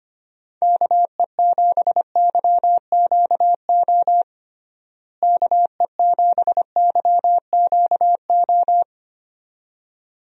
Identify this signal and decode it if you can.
Morse code, heard as a keyed tone, about 25 wpm, 700 Hz. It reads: KE7YQO KE7YQO
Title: USA Call Signs - 6 Characters - Repeated 2x - 25wpm